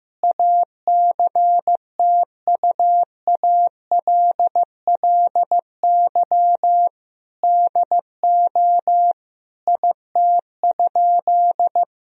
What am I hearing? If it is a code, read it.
Morse code, heard as a keyed tone, about 15 wpm, 700 Hz. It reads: ACTUALLY DO IT?
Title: Sentences from Top 1000 Words - Review Part 4 - 15wpm